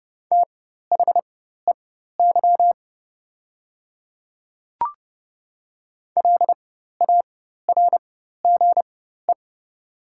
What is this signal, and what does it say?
Morse code, heard as a keyed tone, about 30 wpm, spaced as for 14 wpm, 700 Hz. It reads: THEY LARGE
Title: 4x Character Spacing - Top 200 Words - 30wpm